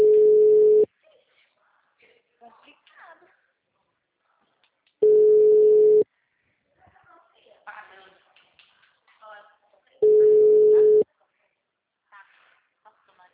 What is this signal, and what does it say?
Ringback tone (Europe, 425 Hz), with speech.
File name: out- -2013- - - .45421